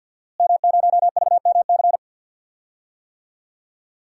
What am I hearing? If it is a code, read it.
Morse code, heard as a keyed tone, about 49 wpm, 700 Hz. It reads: M0VMC
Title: DXCC-100 Call Signs - 5 Characters - Repeated 2x - 50wpm